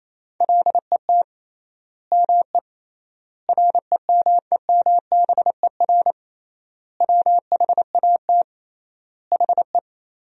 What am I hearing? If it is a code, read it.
Morse code, heard as a keyed tone, about 28 wpm, 700 Hz. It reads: LET ME REMEMBER WHAT HE
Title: 3x Word-Spacing - Sentences from Top 400 Words - 28wpm